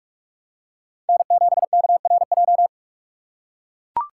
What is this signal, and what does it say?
Morse code, heard as a keyed tone, about 44 wpm, 700 Hz. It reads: N7KRJ